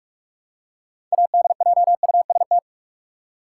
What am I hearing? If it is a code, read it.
Morse code, heard as a keyed tone, about 46 wpm, 700 Hz. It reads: ADJUST
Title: Six Letter Words - 45wpm